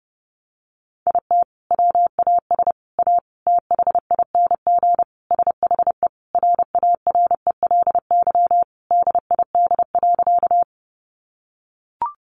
Morse code, heard as a keyed tone, about 30 wpm, 700 Hz. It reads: IT WAS A THING SHE RARELY DID.
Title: Sentences from Top 1400 Words 30wpm